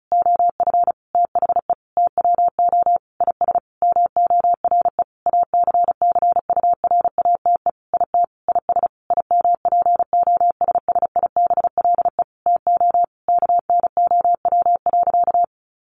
Morse code, heard as a keyed tone, about 34 wpm, 700 Hz. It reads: OF THE TWO IS MORE ACCURATE IT IS IMPOSSIBLE TO KNOW.
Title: Sentences from Top 1200 Words 35wpm